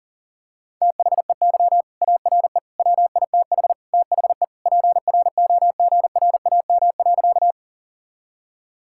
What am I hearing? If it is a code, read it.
Morse code, heard as a keyed tone, about 40 wpm, 700 Hz. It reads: THEY ARE WITH THE PROGRAM.
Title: Sentences from Top 1000 Words - Review Part 5 - 40wpm